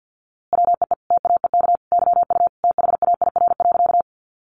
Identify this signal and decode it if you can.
Morse code, heard as a keyed tone, about 50 wpm, 700 Hz. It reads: WEE TAEK YU THAIR.
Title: A Sound Of Thunder 50wpm